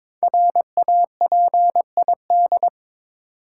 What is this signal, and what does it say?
Morse code, heard as a keyed tone, about 22 wpm, 700 Hz. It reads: RAPID
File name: 5 Letter Words - First 3 Letters Given - 22wpm